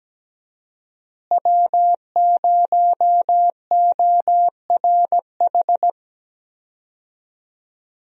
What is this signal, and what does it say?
Morse code, heard as a keyed tone, about 17 wpm, 700 Hz. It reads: W0ORH
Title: USA Call Signs - 5 Characters - Repeated 3x - 17wpm